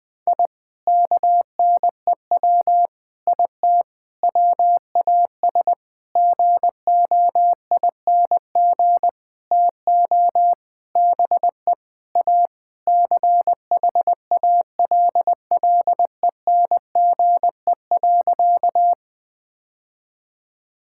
Morse code, heard as a keyed tone, about 21 wpm, 700 Hz. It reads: I KNEW IT WAS GOING TO BE A CHALLENGE.
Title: Sentences with 5-Word Phrases - Spoken Beginning - 20wpm